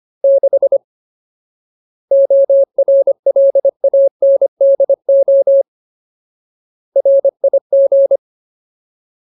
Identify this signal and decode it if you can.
Morse code, heard as a keyed tone, about 25 wpm, 550 Hz. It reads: 6 ORLANDO RIG
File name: Ultimate Word List / 25WPM Ultimate